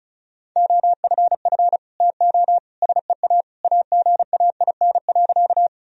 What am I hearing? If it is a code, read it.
Morse code, heard as a keyed tone, about 34 wpm, 700 Hz. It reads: OFF TO SEA AGAIN.
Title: Sentences from Top 1000 Words - Review Part 1 - 35wpm